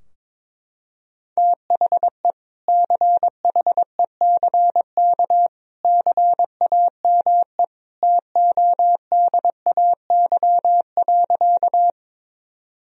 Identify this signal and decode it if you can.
Morse code, heard as a keyed tone, about 22 wpm, 700 Hz. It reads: THE CHECK CAME TODAY.
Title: Sentences from Top 1000 Words - Review Part 3 - 22wpm